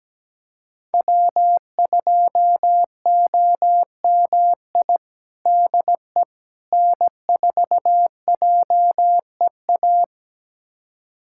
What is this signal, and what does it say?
Morse code, heard as a keyed tone, about 17 wpm, 700 Hz. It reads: W2OMI DE N4JEA
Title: USA Callsign DE Callsign - 5 Characters - 17wpm